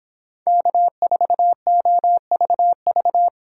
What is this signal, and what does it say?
Morse code, heard as a keyed tone, about 25 wpm, 700 Hz. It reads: K4OVV